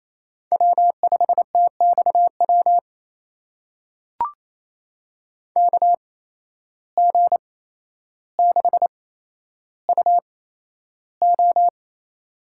Morse code, heard as a keyed tone, about 28 wpm, spaced as for 12 wpm, 700 Hz. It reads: W5TXW KG6UO
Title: USA Call Signs - 5 Characters - 8x Character Spacing - 28wpm